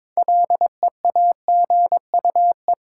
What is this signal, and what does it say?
Morse code, heard as a keyed tone, about 22 wpm, 700 Hz. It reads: LEAGUE
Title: Six Letter Words - 22wpm